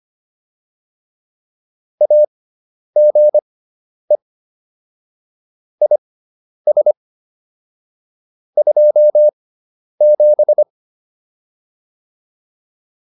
Morse code, heard as a keyed tone, about 25 wpm, spaced as for 10 wpm, 600 Hz. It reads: AGE IS 27